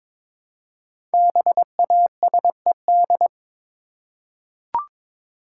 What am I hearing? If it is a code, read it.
Morse code, heard as a keyed tone, about 22 wpm, 700 Hz. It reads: BASED
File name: 5 Letter Words - First 3 Letters Given - 22wpm